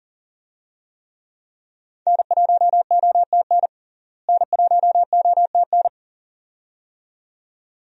Morse code, heard as a keyed tone, about 40 wpm, 700 Hz. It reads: N1OTN N1OTN
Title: USA Call Signs - 5 Characters - Repeated 2x - 40wpm